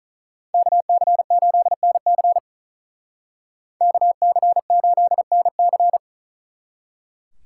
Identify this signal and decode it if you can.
Morse code, heard as a keyed tone, about 46 wpm, 700 Hz. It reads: KC8NC KC8NC